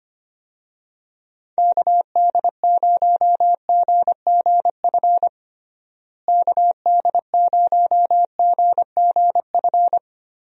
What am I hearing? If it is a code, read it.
Morse code, heard as a keyed tone, about 25 wpm, 700 Hz. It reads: KD0GGF KD0GGF
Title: USA Call Signs - 6 Characters - Repeated 2x - 25wpm